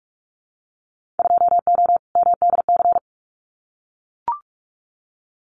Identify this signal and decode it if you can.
Morse code, heard as a keyed tone, about 44 wpm, 700 Hz. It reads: 2O MDC
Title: Winter Field Day Exchanges 45wpm